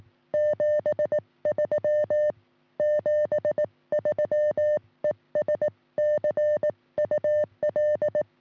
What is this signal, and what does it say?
Morse code, heard as a keyed tone, about 18 wpm, 590 Hz. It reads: 73 73 ES CUL
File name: sm7lcb / 1941 sm0err 10g rs